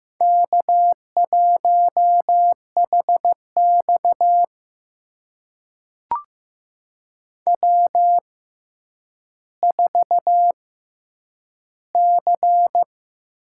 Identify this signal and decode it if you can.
Morse code, heard as a keyed tone, about 15 wpm, spaced as for 9 wpm, 700 Hz. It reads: K1HX W4C
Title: USA Call Signs - 6x Character Spacing - 15wpm